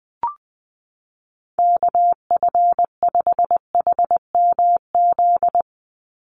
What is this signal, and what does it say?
Morse code, heard as a keyed tone, about 20 wpm, 700 Hz. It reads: KF5HMZ
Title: USA Call Signs 20wpm